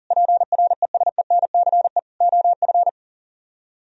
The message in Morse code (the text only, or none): PRESENCE OF